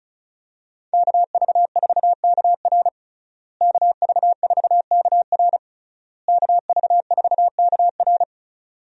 Morse code, heard as a keyed tone, about 36 wpm, 700 Hz. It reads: KV4KR KV4KR KV4KR
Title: USA Call Signs - 5 Characters - Repeated 3x - 35wpm